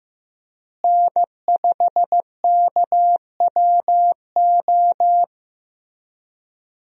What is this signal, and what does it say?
Morse code, heard as a keyed tone, about 15 wpm, 700 Hz. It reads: N5KWO
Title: USA Call Signs - 5 Characters - Repeated 3x - 15wpm